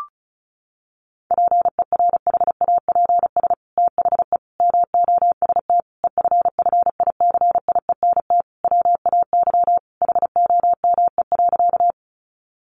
Morse code, heard as a keyed tone, about 34 wpm, 700 Hz. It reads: PERHAPS THE MOST EFFICIENT WAY HOME.
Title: Sentences from Top 1300 Words 35wpm